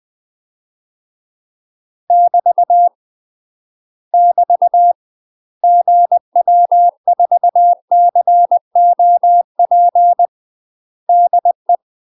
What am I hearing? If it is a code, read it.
Morse code, heard as a keyed tone, about 20 wpm, 700 Hz. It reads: <BT> <BT> GW4COP DE
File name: UK MASTER QSOs TRAINING / 20WPM / UK MASTER QSOs 0072